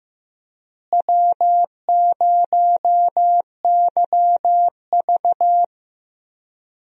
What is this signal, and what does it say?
Morse code, heard as a keyed tone, about 15 wpm, 700 Hz. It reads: W0YV
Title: USA Call Signs - 4 Characters - Repeated 2x - 15wpm